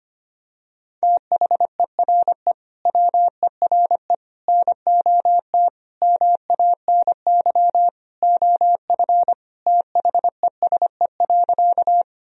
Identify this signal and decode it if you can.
Morse code, heard as a keyed tone, about 25 wpm, 700 Hz. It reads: THERE WERE NOT MANY OF THESE.
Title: Sentences from Top 1000 Words - Review Part 1 - 25wpm